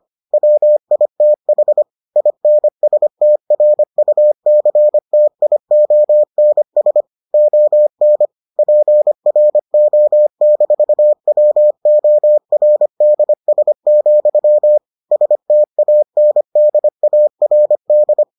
Morse code, heard as a keyed tone, about 25 wpm, 600 Hz. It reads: WITH INSTRUCTIONS ON PRO-WORDS, STANDARD